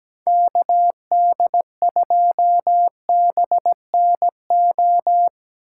Morse code, heard as a keyed tone, about 19 wpm, 700 Hz. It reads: KD2BNO
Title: Speed Racing - USA 2x3 Call Signs 17wpm